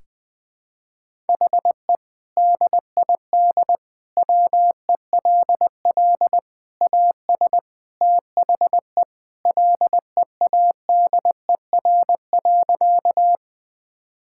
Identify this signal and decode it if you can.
Morse code, heard as a keyed tone, about 20 wpm, 700 Hz. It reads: HE DID WELL AS THE LEADER.
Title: Sentences with 3-Word Phrases - 20wpm